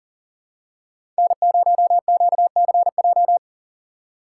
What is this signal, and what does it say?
Morse code, heard as a keyed tone, about 40 wpm, 700 Hz. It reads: N0QCJ
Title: USA Call Signs - 5 Characters - 6x Character Spacing - 40wpm